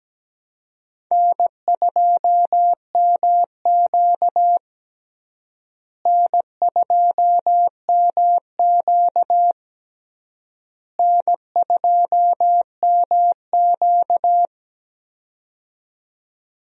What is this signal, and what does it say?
Morse code, heard as a keyed tone, about 17 wpm, 700 Hz. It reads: N2MQ N2MQ N2MQ
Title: USA Call Signs - 4 Characters - Repeated 3x - 17wpm